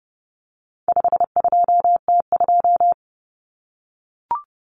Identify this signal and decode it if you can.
Morse code, heard as a keyed tone, about 30 wpm, 700 Hz. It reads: 52T2
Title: Common Cut Numbers - 4-Digit 30wpm